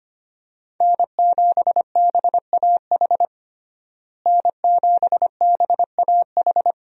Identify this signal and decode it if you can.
Morse code, heard as a keyed tone, about 25 wpm, 700 Hz. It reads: N7BAH N7BAH
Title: USA Call Signs - 5 Characters - Repeated 2x - 25wpm